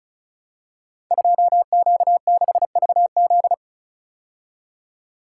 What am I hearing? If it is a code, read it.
Morse code, heard as a keyed tone, about 35 wpm, 700 Hz. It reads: 2Q6VZ
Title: Code Groups - 5 Letter:Numbers - 35wpm